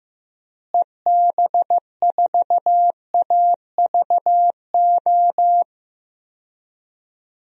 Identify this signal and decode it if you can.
Morse code, heard as a keyed tone, about 14 wpm, 700 Hz. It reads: EB4AVO
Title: DXCC-100 Call Signs - 6 Characters - Repeated 3x - 15wpm